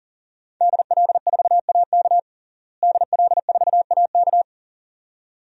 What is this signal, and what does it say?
Morse code, heard as a keyed tone, about 37 wpm, 690 Hz. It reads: DL4AK DL4AK
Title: DXCC-100 Call Signs - 5 Characters - Repeated 2x - 40wpm